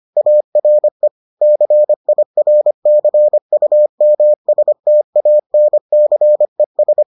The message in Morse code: ARE CIRCUMSTANCES

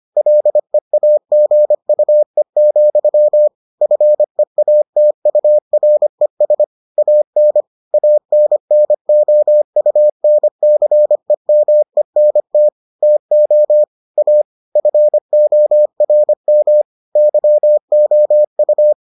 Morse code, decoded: LEAGUE, FEATURES AN ANNOUNCEMENT TO A FORM YOU

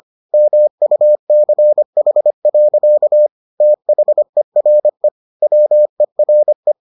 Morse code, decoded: MUCH. THERE WERE

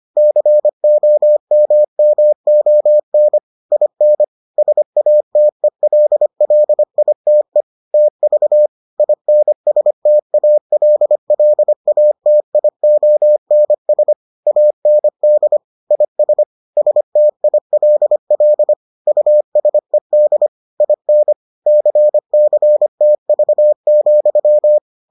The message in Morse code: COMMON IN SATELLITE TV INSTALLATIONS AND IS STILL USED IN CCTV,

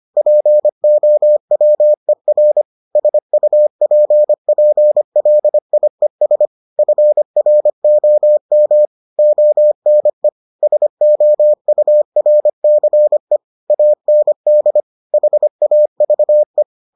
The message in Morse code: POWER SUPPLIES FROM ONE SOURCE AND HAVE

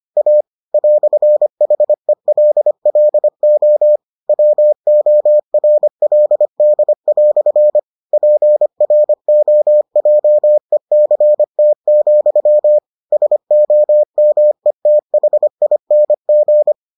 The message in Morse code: A "HELLO WORLD" PROJECT, SOMETHING

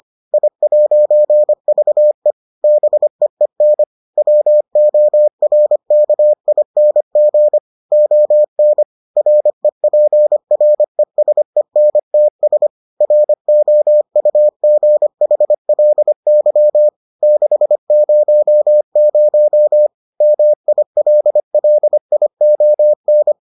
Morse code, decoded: I'VE BEEN WORKING ON REPRESENTS ROUGHLY 600 MILLION

